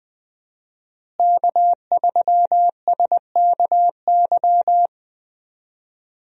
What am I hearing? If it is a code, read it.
Morse code, heard as a keyed tone, about 20 wpm, 700 Hz. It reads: K3SKY
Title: USA Call Signs - 5 Characters - Repeated 2x - 20wpm